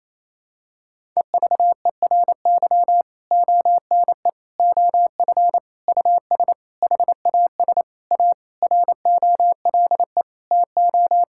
Morse code, heard as a keyed tone, about 28 wpm, 700 Hz. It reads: EVERY ONE OF US HAS A ROLE TO
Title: Sentences from Top 1000 Words - Review Part 6 - 28wpm